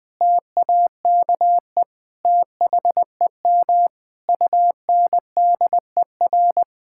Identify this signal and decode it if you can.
Morse code, heard as a keyed tone, about 20 wpm, 700 Hz. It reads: TAKE THEM UNDER
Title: Sentences from Top 1600 Words 20wpm